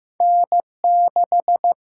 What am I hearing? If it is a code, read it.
Morse code, heard as a keyed tone, about 15 wpm, 700 Hz. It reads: N6
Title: USA Call Signs - 4 Characters - 6x Character Spacing - 15wpm